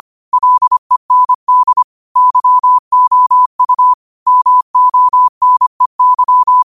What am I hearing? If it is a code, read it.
Morse code, heard as a keyed tone, about 25 wpm, 1000 Hz. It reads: LEND YOU MONEY